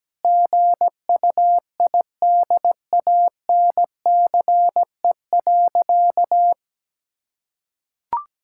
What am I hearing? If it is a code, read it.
Morse code, heard as a keyed tone, about 17 wpm, 700 Hz. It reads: GUIDANCE.
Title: Sentences from Top 1600 Words 17wpm